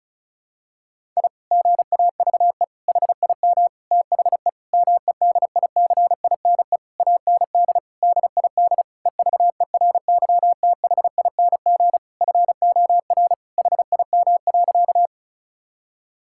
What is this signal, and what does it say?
Morse code, heard as a keyed tone, about 36 wpm, 700 Hz. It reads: I GAVE HIM THE MEDICINE AND DID EVERYTHING FOR HIM.
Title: Sentences from Top 1000 Words - Review Part 6 - 35wpm